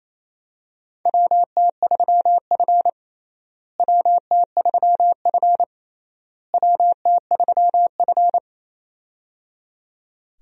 Morse code, heard as a keyed tone, about 28 wpm, 700 Hz. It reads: WT3F WT3F WT3F
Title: USA Call Signs - 4 Characters - Repeated 3x - 28wpm